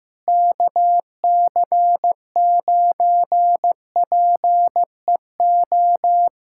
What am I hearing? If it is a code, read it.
Morse code, heard as a keyed tone, about 14 wpm, 700 Hz. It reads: KC9PEO
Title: USA Callsign DE Callsign 15wpm